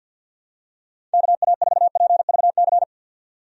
Morse code, heard as a keyed tone, about 50 wpm, 700 Hz. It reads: KA4PVC
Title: USA Call Signs - 6 Characters - 50wpm